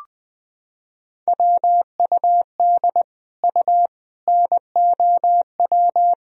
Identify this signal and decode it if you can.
Morse code, heard as a keyed tone, about 20 wpm, 700 Hz. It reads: WUD U NOW
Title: Sentences from Top 200 Words and Abbreviations 20wpm